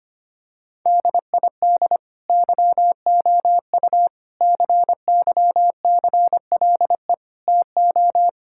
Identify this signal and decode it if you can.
Morse code, heard as a keyed tone, about 25 wpm, 700 Hz. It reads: DID YOU CYCLE TO
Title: Sentences from Top 1000 Words - Review Part 6 - 25wpm